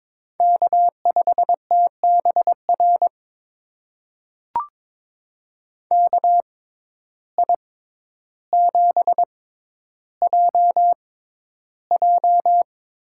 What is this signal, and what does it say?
Morse code, heard as a keyed tone, about 22 wpm, spaced as for 11 wpm, 700 Hz. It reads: K5TBR KI7JJ
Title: USA Call Signs - 6x Character Spacing - 22wpm